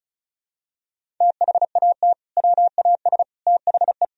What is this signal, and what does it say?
Morse code, heard as a keyed tone, about 34 wpm, 700 Hz. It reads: THAT WAS THE
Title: Sentences from Top 1000 Words - Review Part 2 - 35wpm